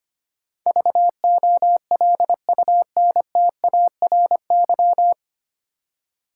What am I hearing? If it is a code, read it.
Morse code, heard as a keyed tone, about 25 wpm, 700 Hz. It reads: VOLUNTARY
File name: practice / 25wpm / 9 Letter Words - First 3 Letters Given - 25wpm